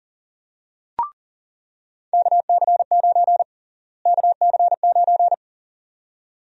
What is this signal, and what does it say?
Morse code, heard as a keyed tone, about 40 wpm, 700 Hz. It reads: KC9 KC9
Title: USA Call Sign Prefixes - Repeated 2x - 40wpm